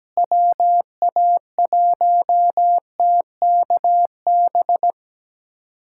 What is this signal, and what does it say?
Morse code, heard as a keyed tone, about 17 wpm, 700 Hz. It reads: WA1TKB
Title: USA Callsign DE Callsign - 6 Characters - 17wpm